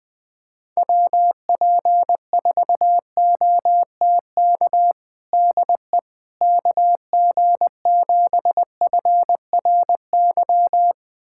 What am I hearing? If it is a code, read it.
Morse code, heard as a keyed tone, about 20 wpm, 700 Hz. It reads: WP4OTK DE KG7FRY
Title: USA Callsign DE Callsign - 6 Characters - 20wpm